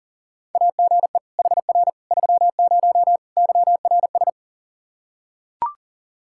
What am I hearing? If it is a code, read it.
Morse code, heard as a keyed tone, about 40 wpm, 700 Hz. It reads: AGE HR 30 YRS
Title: Practice QSO - Exchange 3 - More Info - 40wpm